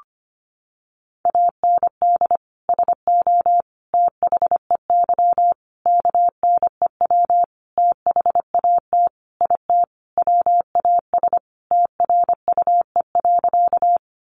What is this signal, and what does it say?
Morse code, heard as a keyed tone, about 25 wpm, 700 Hz. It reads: AND SO THEY KNEW THAT IT WAS TRUE.
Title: Sentences from Top 400 Words 25wpm